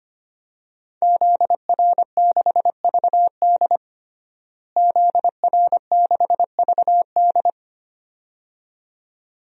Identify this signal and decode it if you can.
Morse code, heard as a keyed tone, about 25 wpm, 700 Hz. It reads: ZR6VD ZR6VD
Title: DXCC-100 Call Signs - 5 Characters - Repeated 2x - 25wpm